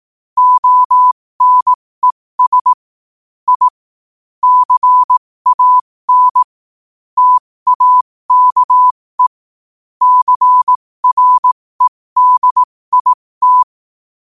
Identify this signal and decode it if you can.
Morse code, heard as a keyed tone, about 18 wpm, spaced as for 15 wpm, 1000 Hz. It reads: ONES I CAN TAKE CREDIT
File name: qotdfiles / 15wpm